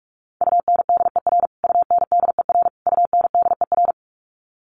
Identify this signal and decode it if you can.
Morse code, heard as a keyed tone, about 44 wpm, 700 Hz. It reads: UNDER UNDER UNDER
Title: Top 300 Words Repeated 45wpm